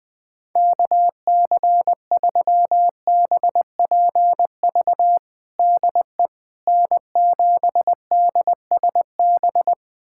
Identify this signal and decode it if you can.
Morse code, heard as a keyed tone, about 20 wpm, 700 Hz. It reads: KC3BPV DE N7DSB
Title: USA Callsign DE Callsign 20wpm